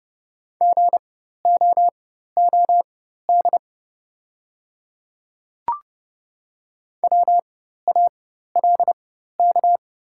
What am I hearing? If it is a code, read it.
Morse code, heard as a keyed tone, about 30 wpm, spaced as for 14 wpm, 700 Hz. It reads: GOOD WALK
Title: 4x Character Spacing - Top 300 Words - 30wpm